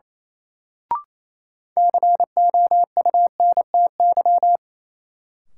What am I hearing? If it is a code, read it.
Morse code, heard as a keyed tone, about 28 wpm, 700 Hz. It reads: COUNTY
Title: Sentences from Top 1200 Words - New Word Drill - 28wpm